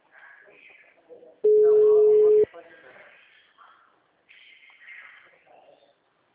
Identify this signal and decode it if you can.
Ringback tone (Europe, 425 Hz), with speech.
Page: Itapetininga